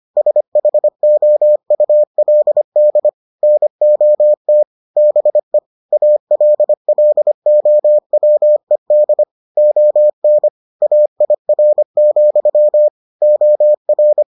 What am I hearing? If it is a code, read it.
Morse code, heard as a keyed tone, about 25 wpm, 600 Hz. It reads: SHOULD NOT BE ALLOWED ON AIR, OR